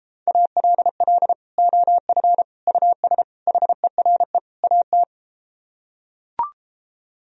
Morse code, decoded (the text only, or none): ALL OF US HERE AT